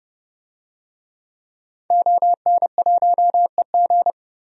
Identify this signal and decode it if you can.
Morse code, heard as a keyed tone, about 30 wpm, 700 Hz. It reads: ON1EG
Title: DXCC-100 Call Signs - 5 Characters - 30wpm